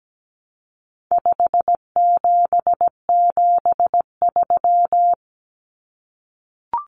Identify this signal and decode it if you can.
Morse code, heard as a keyed tone, about 17 wpm, 700 Hz. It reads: 5773
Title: Common Cut Numbers - 4-Digit 17wpm